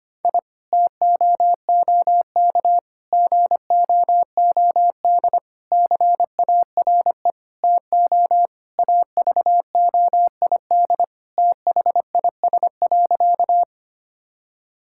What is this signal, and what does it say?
Morse code, heard as a keyed tone, about 25 wpm, 700 Hz. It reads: I TOOK GOOD CARE TO AVOID THIS.
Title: Sentences from Top 1000 Words - Review Part 6 - 25wpm